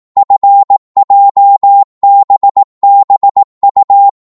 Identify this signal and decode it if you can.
Morse code, heard as a keyed tone, about 18 wpm, 800 Hz. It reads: FJBBU